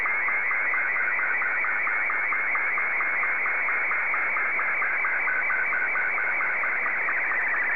SSTV (slow-scan television), Martin 2.
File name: sstv / M2-ZL2TJM